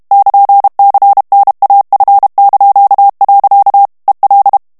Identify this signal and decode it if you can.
Morse code, heard as a keyed tone, about 32 wpm, 800 Hz. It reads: <KN>CNAF). EL
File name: ESPAGNOL TEXTE EN CLAIR 2 VITESSE 30 MOTS MINUTES